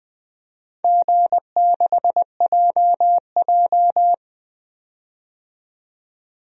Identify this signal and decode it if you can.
Morse code, heard as a keyed tone, about 20 wpm, 700 Hz. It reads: G6JJ
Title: DXCC-100 Call Signs - 4 Characters - 20wpm